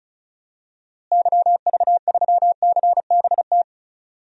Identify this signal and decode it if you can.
Morse code, heard as a keyed tone, about 36 wpm, 700 Hz. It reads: YV3CBT
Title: DXCC-100 Call Signs - 6 Characters - Repeated 3x - 35wpm